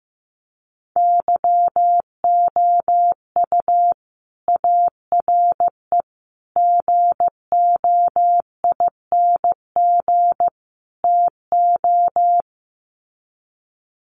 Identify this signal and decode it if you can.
Morse code, heard as a keyed tone, about 15 wpm, 700 Hz. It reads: YOU ARE GOING TO
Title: Sets of 4 Words 15wpm